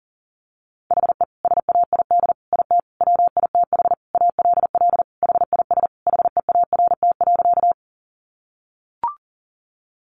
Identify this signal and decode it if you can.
Morse code, heard as a keyed tone, about 40 wpm, 700 Hz. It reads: HE SAID IT WITH ALL HIS HEART.
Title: Sentences from Top 1200 Words 40wpm